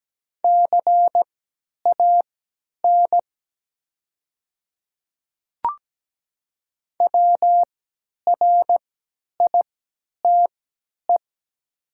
Morse code, heard as a keyed tone, about 17 wpm, spaced as for 10 wpm, 700 Hz. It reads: CAN WRITE
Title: 3x Character Spacing - Top 100 Words - 17wpm